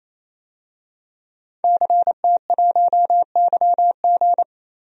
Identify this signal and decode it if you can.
Morse code, heard as a keyed tone, about 28 wpm, 700 Hz. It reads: CT1YG
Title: DXCC-100 Call Signs - 5 Characters - 28wpm